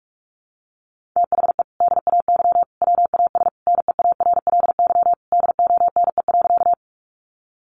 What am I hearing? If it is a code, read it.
Morse code, heard as a keyed tone, about 44 wpm, 700 Hz. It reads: THE DAY WAS NEARLY DONE.
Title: Sentences from Top 1300 Words 45wpm